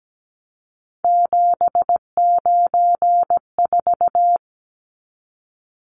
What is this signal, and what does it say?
Morse code, heard as a keyed tone, about 17 wpm, 700 Hz. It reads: 794
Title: Numbers - 3-Digit 17wpm